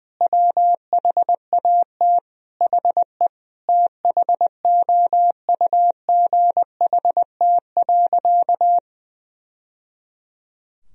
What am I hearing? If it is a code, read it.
Morse code, heard as a keyed tone, about 20 wpm, 700 Hz. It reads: WHAT HE THOUGHT.
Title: Sentences from Top 1000 Words - Review Part 2 - 20wpm